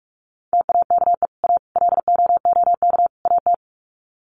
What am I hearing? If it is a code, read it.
Morse code, heard as a keyed tone, about 44 wpm, 700 Hz. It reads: TAKE A LOOK AT
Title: Sets of 4 Words 45wpm